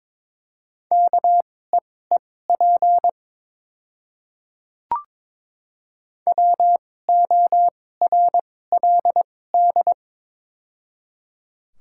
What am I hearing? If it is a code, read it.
Morse code, heard as a keyed tone, about 23 wpm, spaced as for 16 wpm, 700 Hz. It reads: KEEP WORLD
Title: 2x Character Spacing - Top 500 Words - 22wpm